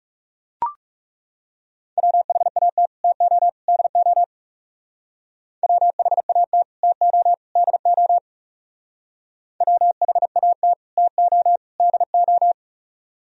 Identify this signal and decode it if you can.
Morse code, heard as a keyed tone, about 36 wpm, 700 Hz. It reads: WHAT TO DO WHAT TO DO WHAT TO DO
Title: Speed-Racing - Sets of 3 Words 30wpm